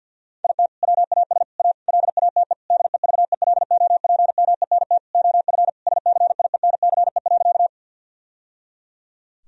Morse code, heard as a keyed tone, about 50 wpm, 700 Hz. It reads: IT WAS A LATE DEVELOPMENT OF SCIENCE.
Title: Sentences from Top 1000 Words - Review Part 4 - 50wpm